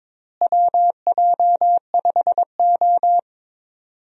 Morse code, heard as a keyed tone, about 22 wpm, 700 Hz. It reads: WJ5O
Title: USA Call Signs - 4 Characters - Repeated 3x - 22wpm